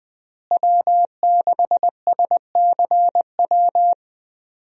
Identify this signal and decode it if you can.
Morse code, heard as a keyed tone, about 20 wpm, 700 Hz. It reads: W6SCW
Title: USA Call Signs - 5 Characters - Repeated 2x - 20wpm